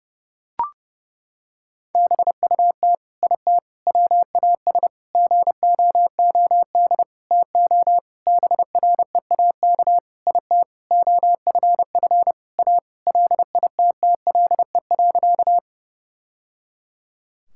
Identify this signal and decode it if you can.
Morse code, heard as a keyed tone, about 30 wpm, 700 Hz. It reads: BUT IT WAS GOOD TO BREAK IT OFF A LITTLE.
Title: Sentences from Top 1000 Words - Review Part 6 - 30wpm